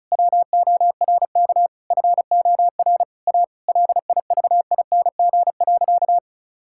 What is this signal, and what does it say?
Morse code, heard as a keyed tone, about 36 wpm, 700 Hz. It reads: WORK FOR A LIVING.
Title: Sentences from Top 1000 Words - Review Part 6 - 35wpm